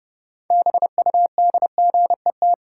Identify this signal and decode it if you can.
Morse code, heard as a keyed tone, about 30 wpm, 700 Hz. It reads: BUDGET
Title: Sentences from Top 1200 Words - New Word Drill - 30wpm